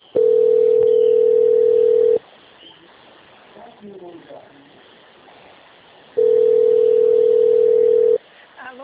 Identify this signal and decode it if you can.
Ringback tone (North America), 440 + 480 Hz, with speech.
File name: out- - - - - .14024